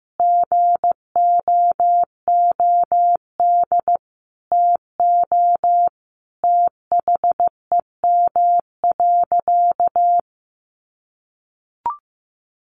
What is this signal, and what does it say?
Morse code, heard as a keyed tone, about 15 wpm, 700 Hz. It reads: GOOD TO THEM.
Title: Sentences from Top 1200 Words 15wpm